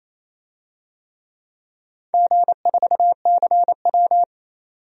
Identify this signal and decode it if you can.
Morse code, heard as a keyed tone, about 28 wpm, 700 Hz. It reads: G4CW
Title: DXCC-100 Call Signs - 4 Characters - 28wpm